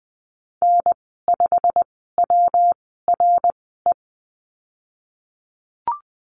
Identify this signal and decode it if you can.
Morse code, heard as a keyed tone, about 20 wpm, spaced as for 14 wpm, 700 Hz. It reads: N5WRE
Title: 186 Lesson 30g - Letter b - Call Signs 20wpm